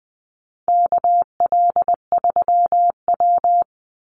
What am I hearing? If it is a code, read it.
Morse code, heard as a keyed tone, about 20 wpm, 700 Hz. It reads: KL3W
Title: Call Signs Repeated 2x - 20wpm